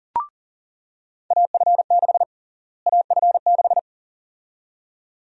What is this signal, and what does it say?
Morse code, heard as a keyed tone, about 40 wpm, 700 Hz. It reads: AF6 AF6
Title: USA Call Sign Prefixes - Repeated 2x - 40wpm